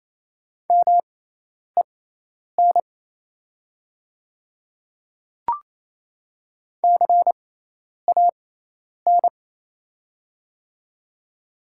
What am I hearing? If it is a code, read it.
Morse code, heard as a keyed tone, about 28 wpm, spaced as for 10 wpm, 690 Hz. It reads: MEN CAN
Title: 6x Character Spacing - Top 200 Words - 28wpm